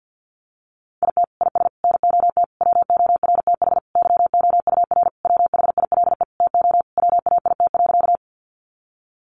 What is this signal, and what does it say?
Morse code, heard as a keyed tone, about 50 wpm, 700 Hz. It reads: IT IS NOT WORTH YOUR WHILE TO WAIT.
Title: Sentences from Top 800 Words 50wpm